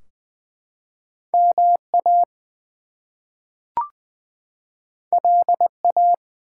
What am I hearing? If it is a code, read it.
Morse code, heard as a keyed tone, about 20 wpm, 700 Hz. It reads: MA LA
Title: US States and Canadian Provinces 20wpm